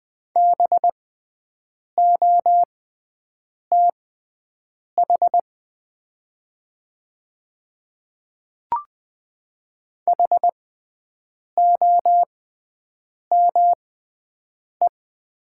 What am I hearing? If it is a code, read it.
Morse code, heard as a keyed tone, about 20 wpm, spaced as for 7 wpm, 700 Hz. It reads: BOTH HOME
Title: 6x Character Spacing - Top 300 Words - 20wpm